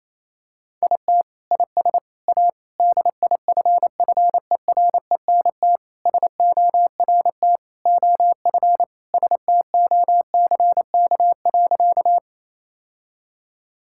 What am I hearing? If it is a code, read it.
Morse code, heard as a keyed tone, about 28 wpm, 700 Hz. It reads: IT IS A DIFFERENT SORT OF STOCK.
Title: Sentences from Top 1000 Words - Review Part 5 - 28wpm